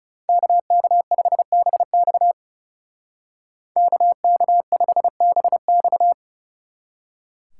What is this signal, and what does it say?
Morse code, heard as a keyed tone, about 36 wpm, 700 Hz. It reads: KK5BX KK5BX
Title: Speed Racing - USA Call Signs 30wpm